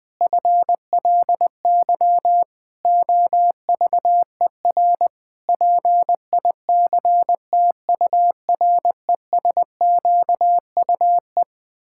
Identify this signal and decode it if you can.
Morse code, heard as a keyed tone, about 21 wpm, 700 Hz. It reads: FLY OVER PICTURESQUE
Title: Pangram Challenge v2 - 20wpm